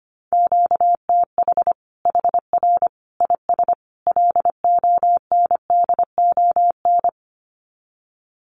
Morse code, heard as a keyed tone, about 25 wpm, 700 Hz. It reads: QTH HR IS LONDON
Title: Common QSO Phrases 25wpm